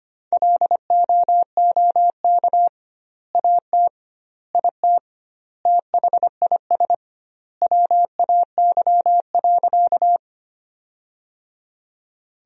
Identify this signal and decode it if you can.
Morse code, heard as a keyed tone, about 25 wpm, 700 Hz. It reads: LOOK AT IT THIS WAY.
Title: 2x Word-Spacing - Sentences from Top 100 Words - 25wpm